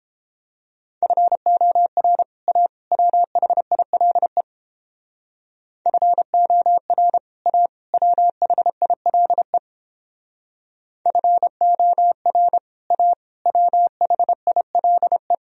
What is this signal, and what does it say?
Morse code, heard as a keyed tone, about 29 wpm, 700 Hz. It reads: FOR A WHILE FOR A WHILE FOR A WHILE